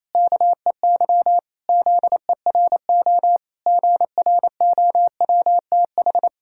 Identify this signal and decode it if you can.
Morse code, heard as a keyed tone, about 27 wpm, 700 Hz. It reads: KEY ZERO GROWTH
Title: Pangram Challenge v2 - 28wpm